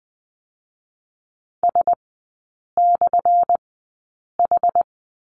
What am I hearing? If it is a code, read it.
Morse code, heard as a keyed tone, about 19 wpm, spaced as for 8 wpm, 690 Hz. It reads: S/H
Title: 212 Lesson 34e - Number 8 - Three Characters 20wpm